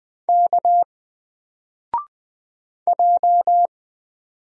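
Morse code, heard as a keyed tone, about 20 wpm, 700 Hz. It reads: K J
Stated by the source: Single letter and number Morse drill at 20 wpm, from N4LCD.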